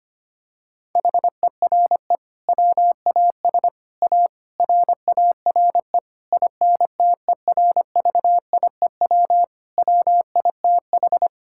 Morse code, heard as a keyed tone, about 25 wpm, 700 Hz. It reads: HERE WAS A RARE INTERVIEW WITH
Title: Sentences from Top 1600 Words 25wpm